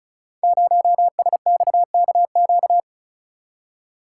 Morse code, heard as a keyed tone, about 36 wpm, 700 Hz. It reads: 0SXKQ
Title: Code Groups - 5 Letter:Numbers - 35wpm